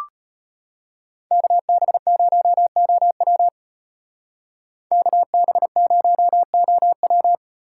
Morse code, heard as a keyed tone, about 31 wpm, 700 Hz. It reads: KB0OW KB0OW
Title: Speed Racing - USA 2x2 Call Signs 25wpm